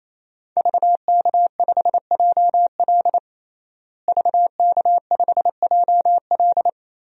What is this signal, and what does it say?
Morse code, heard as a keyed tone, about 28 wpm, 700 Hz. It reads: VK5JL VK5JL
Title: DXCC-100 Call Signs - 5 Characters - Repeated 3x - 28wpm